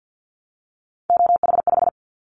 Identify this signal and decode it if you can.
Morse code, heard as a keyed tone, about 50 wpm, 700 Hz. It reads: OH5
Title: Callsign Prefixes for Top 50 DXCC - 50wpm